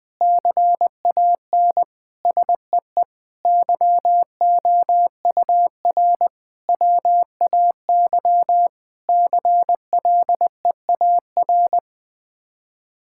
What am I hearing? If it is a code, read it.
Morse code, heard as a keyed tone, about 20 wpm, 700 Hz. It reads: CAN SEE YOUR WAY CLEAR
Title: Sets of 8 Words - 20wpm